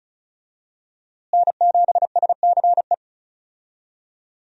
Morse code, heard as a keyed tone, about 34 wpm, 700 Hz. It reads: N7SCE
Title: USA Call Signs - 5 Characters - Repeated 3x - 35wpm